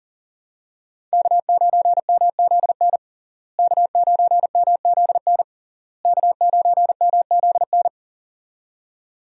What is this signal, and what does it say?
Morse code, heard as a keyed tone, about 40 wpm, 700 Hz. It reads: K9MZN K9MZN K9MZN
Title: USA Call Signs - 5 Characters - Repeated 3x - 40wpm